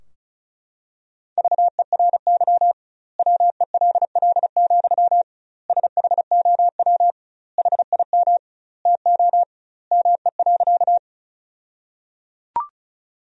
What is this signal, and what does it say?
Morse code, heard as a keyed tone, about 34 wpm, 700 Hz. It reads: VERY WELL, SHOW HIM TO ME.
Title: 2x Word-Spacing - Sentences from Top 200 Words - 35wpm